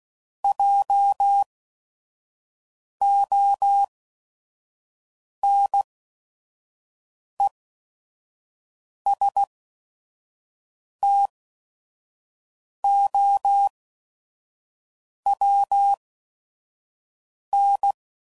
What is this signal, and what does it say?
Morse code, heard as a keyed tone, about 16 wpm, spaced as for 5 wpm, 800 Hz. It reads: JONESTOWN
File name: CW Code Groups @5WPM / fromSM's / Cities